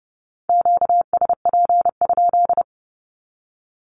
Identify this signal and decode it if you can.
Morse code, heard as a keyed tone, about 30 wpm, 700 Hz. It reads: QSP?
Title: Q-Codes with Spoken Meaning - 30wpm